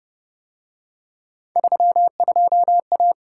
Morse code, heard as a keyed tone, about 32 wpm, 700 Hz. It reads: 32A